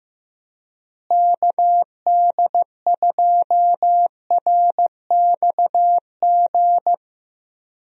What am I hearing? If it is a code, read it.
Morse code, heard as a keyed tone, about 15 wpm, 700 Hz. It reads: KD2RXG
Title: USA Call Signs - 6 Characters - Repeated 2x - 15wpm